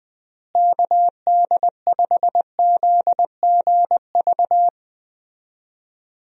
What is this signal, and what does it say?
Morse code, heard as a keyed tone, about 20 wpm, 700 Hz. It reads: KD5ZGV
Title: USA Call Signs - 6 Characters - Repeated 2x - 20wpm